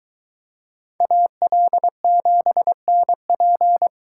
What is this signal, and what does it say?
Morse code, heard as a keyed tone, about 22 wpm, 700 Hz. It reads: AL7NP